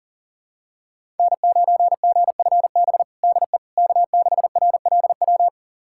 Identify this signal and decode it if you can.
Morse code, heard as a keyed tone, about 40 wpm, 700 Hz. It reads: N9GFB DE K6RLW
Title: USA Callsign DE Callsign - 5 Characters - 40wpm